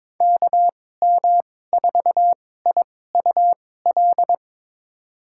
Morse code, heard as a keyed tone, about 22 wpm, spaced as for 16 wpm, 700 Hz. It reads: KM4IUL
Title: USA Call Signs - 2x Character Spacing - 22wpm